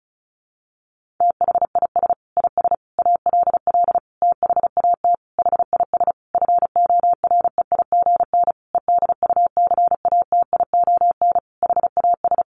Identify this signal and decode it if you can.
Morse code, heard as a keyed tone, about 34 wpm, 700 Hz. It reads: THIS IS ALL THAT HIS FOREIGN EDUCATION HAS
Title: Sentences from Top 1400 Words 35wpm